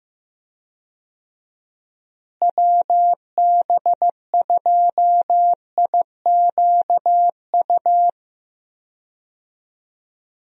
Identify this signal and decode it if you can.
Morse code, heard as a keyed tone, about 15 wpm, 700 Hz. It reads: WB2IQU
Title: USA Call Signs - 6 Characters - Repeated 3x - 15wpm